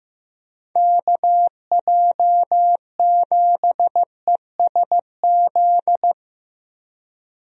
Morse code, heard as a keyed tone, about 15 wpm, 700 Hz. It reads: KJ7ESZ
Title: USA Call Signs - 6 Characters - 15wpm